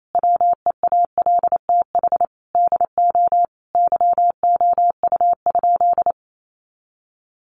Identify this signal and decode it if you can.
Morse code, heard as a keyed tone, about 28 wpm, 700 Hz. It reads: WEALTH DO YOU?
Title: Sentences from Top 1200 Words 28wpm